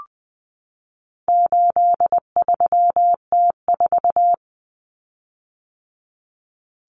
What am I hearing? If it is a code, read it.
Morse code, heard as a keyed tone, about 20 wpm, 700 Hz. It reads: 83T4
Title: Common Cut Numbers - 4-Digit 20wpm